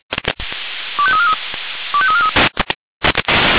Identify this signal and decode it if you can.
Five-tone selective call (ZVEI), heard as quick runs of five tones, tones of about 70 ms. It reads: 25343 25343